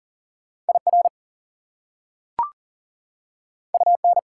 Morse code, heard as a keyed tone, about 40 wpm, 700 Hz. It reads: IR UN
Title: Prefixes and Suffixes 40wpm